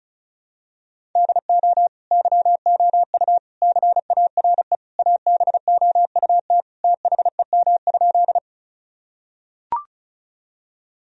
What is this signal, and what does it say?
Morse code, heard as a keyed tone, about 34 wpm, 700 Hz. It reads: DO YOU CARE ABOUT THEM?
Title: Sentences from Top 1000 Words - Review Part 1 - 35wpm